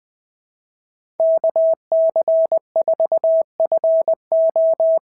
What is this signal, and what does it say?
Morse code, heard as a keyed tone, about 20 wpm, 660 Hz. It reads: KC4FO
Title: SOTA QSO Practice 20wpm